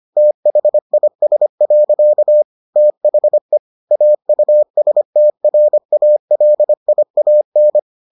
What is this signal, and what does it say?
Morse code, heard as a keyed tone, about 25 wpm, 600 Hz. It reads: THIS. THE AUSTRALIAN